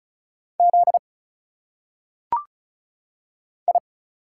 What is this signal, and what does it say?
Morse code, heard as a keyed tone, about 34 wpm, 700 Hz. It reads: Z I
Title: Single Letters - Phonetic Pronunciation 35wpm